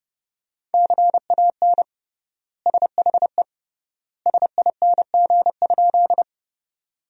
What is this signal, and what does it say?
Morse code, heard as a keyed tone, about 32 wpm, 700 Hz. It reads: CAN SHE SING?
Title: 3x Word-Spacing - Sentences from Top 400 Words - 30wpm